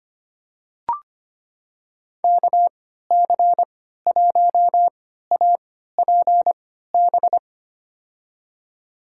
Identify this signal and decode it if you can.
Morse code, heard as a keyed tone, about 25 wpm, spaced as for 14 wpm, 700 Hz. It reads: KC1APB
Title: USA Call Signs - 3x Character Spacing - 25wpm